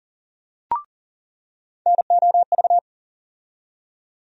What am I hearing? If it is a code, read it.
Morse code, heard as a keyed tone, about 40 wpm, 700 Hz. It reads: NOV
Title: USA Call Sign Postfixes - 3 Letters 40wpm